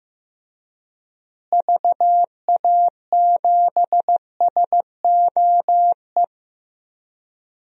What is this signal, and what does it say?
Morse code, heard as a keyed tone, about 16 wpm, 700 Hz. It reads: VA7SOE
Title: DXCC-100 Call Signs - 6 Characters - Repeated 2x - 15wpm